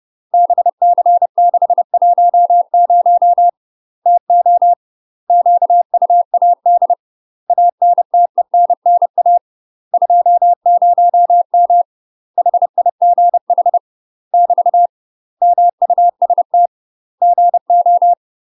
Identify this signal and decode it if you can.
Morse code, heard as a keyed tone, about 30 wpm, 700 Hz. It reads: BC610 TO QUAD ANTENNA 20M HIGH <BT> MUST GO
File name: UK MASTER QSOs TRAINING / 30WPM / UK MASTER QSOs 0022